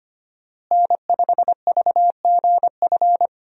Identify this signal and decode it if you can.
Morse code, heard as a keyed tone, about 25 wpm, 700 Hz. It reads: N5VGF
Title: USA Call Signs - 5 Characters - Repeated 3x - 25wpm